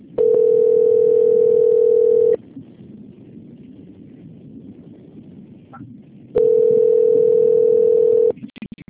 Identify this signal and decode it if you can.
Ringback tone (North America), 440 + 480 Hz.